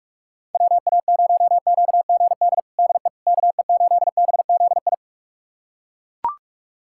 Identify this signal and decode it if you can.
Morse code, heard as a keyed tone, about 41 wpm, 700 Hz. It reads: WA0QGD DE KE8BZI
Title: USA Callsign DE Callsign 45wpm